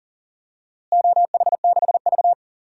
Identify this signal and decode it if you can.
Morse code, heard as a keyed tone, about 40 wpm, 700 Hz. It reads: OH6V
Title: DXCC-100 Call Signs - 4 Characters - Repeated 2x - 40wpm